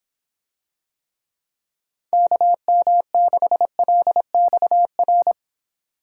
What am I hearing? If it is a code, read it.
Morse code, heard as a keyed tone, about 26 wpm, 700 Hz. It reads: KM6LXR